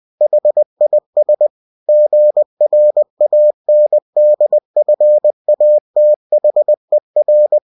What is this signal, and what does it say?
Morse code, heard as a keyed tone, about 20 wpm, 600 Hz. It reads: HIS GRANDFATHER
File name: cw-books / winnie-the-pooh / winnie-20wpm-page-0020